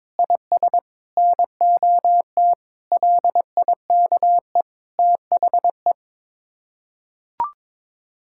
Morse code, heard as a keyed tone, about 22 wpm, 700 Hz. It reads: IS NOT LIKE THE
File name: practice / 22wpm / Sets of 8 Words - 22wpm